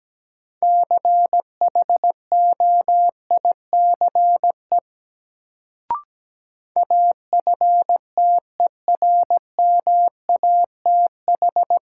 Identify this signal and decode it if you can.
Morse code, heard as a keyed tone, about 17 wpm, 700 Hz. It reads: CHOICE AFTERMATH
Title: Sets of 5 Words - New Word Drill - 17wpm